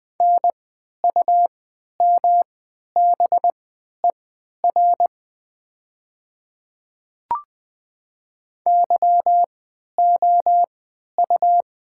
Morse code, heard as a keyed tone, about 20 wpm, spaced as for 11 wpm, 700 Hz. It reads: NUMBER YOU
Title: 3x Character Spacing - Top 100 Words - 20wpm